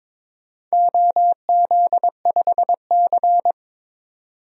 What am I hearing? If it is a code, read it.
Morse code, heard as a keyed tone, about 22 wpm, 700 Hz. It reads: OZ5C
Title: DXCC-100 Call Signs - 4 Characters - 22wpm